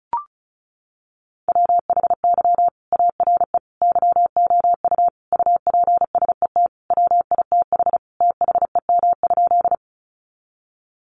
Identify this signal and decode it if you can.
Morse code, heard as a keyed tone, about 34 wpm, 700 Hz. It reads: WHY ARE YOU UPSET WITH THEM?
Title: Sentences from Top 800 Words 35wpm